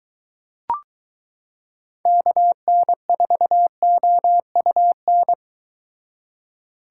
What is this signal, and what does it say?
Morse code, heard as a keyed tone, about 22 wpm, 700 Hz. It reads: KN4OUN